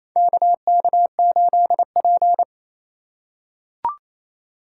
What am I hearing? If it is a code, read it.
Morse code, heard as a keyed tone, about 28 wpm, 700 Hz. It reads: KK8P
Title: USA Call Signs - 4 Characters - 2x Character Spacing - 28wpm